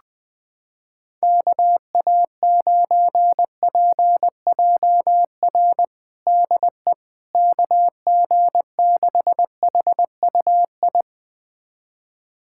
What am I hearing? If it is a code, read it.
Morse code, heard as a keyed tone, about 20 wpm, 700 Hz. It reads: KA9PJR DE KG6HUI
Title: USA Callsign DE Callsign 20wpm